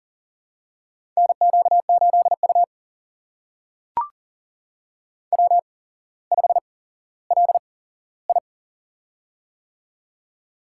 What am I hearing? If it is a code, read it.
Morse code, heard as a keyed tone, about 39 wpm, spaced as for 17 wpm, 700 Hz. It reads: NQ8U W5LI
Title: USA Call Signs - 4 Characters - 8x Character Spacing - 40wpm